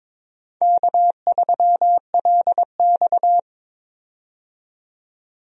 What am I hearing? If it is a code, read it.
Morse code, heard as a keyed tone, about 22 wpm, 700 Hz. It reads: K3LX
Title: USA Call Signs - 4 Characters - 22wpm